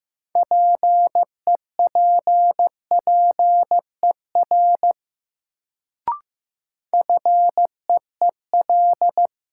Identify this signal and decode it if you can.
Morse code, heard as a keyed tone, about 15 wpm, 700 Hz. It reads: PEPPER FEEL
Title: Sets of 5 Words - New Word Drill - 15wpm